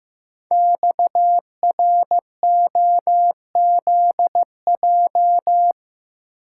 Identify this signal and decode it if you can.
Morse code, heard as a keyed tone, about 15 wpm, 690 Hz. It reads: XROZJ
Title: Code Groups - 5 Letters - 15wpm